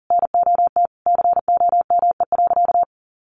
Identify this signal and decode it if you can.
Morse code, heard as a keyed tone, about 40 wpm, 700 Hz. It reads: NOT COME.
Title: Sentences from Top 1500 Words 40wpm